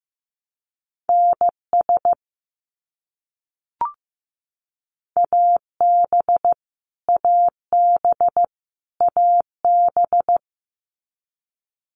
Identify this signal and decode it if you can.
Morse code, heard as a keyed tone, about 15 wpm, 700 Hz. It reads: NS AB AB AB
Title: Canadian Provinces and Territories - Repeated 3x - 15wpm